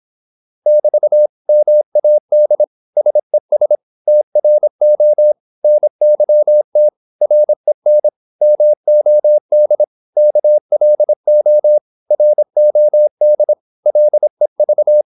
Morse code, heard as a keyed tone, about 26 wpm, 600 Hz. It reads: <BT> MAD SES TRO NYT REN MOD KLO ROD LEV
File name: Korte ord DK 26wpm